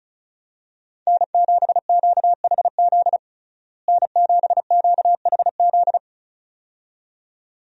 Morse code, heard as a keyed tone, about 34 wpm, 700 Hz. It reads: N7QHZ N7QHZ
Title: USA Call Signs - 5 Characters - Repeated 2x - 35wpm